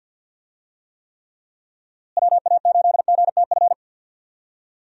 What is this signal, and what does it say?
Morse code, heard as a keyed tone, about 52 wpm, 700 Hz. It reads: WA8GTF
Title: USA Call Signs - 6 Characters - Repeated 3x - 50wpm